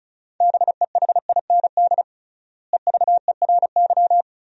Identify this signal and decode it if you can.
Morse code, heard as a keyed tone, about 36 wpm, 700 Hz. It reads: BEHIND EVERY
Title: 3x Word-Spacing - Sentences from Top 500 Words - 35wpm